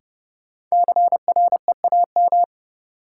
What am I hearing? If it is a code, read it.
Morse code, heard as a keyed tone, about 30 wpm, 700 Hz. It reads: CREAM
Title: Sentences from Top 1500 Words - New Word Drill - 30wpm